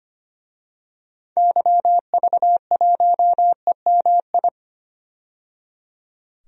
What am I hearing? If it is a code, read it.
Morse code, heard as a keyed tone, about 25 wpm, 700 Hz. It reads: YV1EMI
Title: DXCC-100 Call Signs - 6 Characters - 25wpm